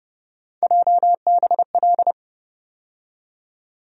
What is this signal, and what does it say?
Morse code, heard as a keyed tone, about 30 wpm, 700 Hz. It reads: JBL
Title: USA Call Sign Postfixes - 3 Letters 30wpm